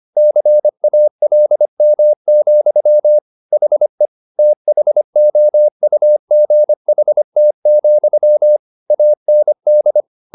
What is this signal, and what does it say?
Morse code, heard as a keyed tone, about 25 wpm, 600 Hz. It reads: CALM, HE THOUGHT, AND
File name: the old man and the sea part 1 25wpm